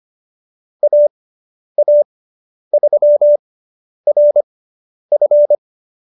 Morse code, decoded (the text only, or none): AA3RF